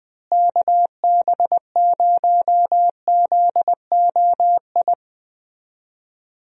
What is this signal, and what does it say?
Morse code, heard as a keyed tone, about 20 wpm, 700 Hz. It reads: KB0ZOI
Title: USA Call Signs - 6 Characters - Repeated 2x - 20wpm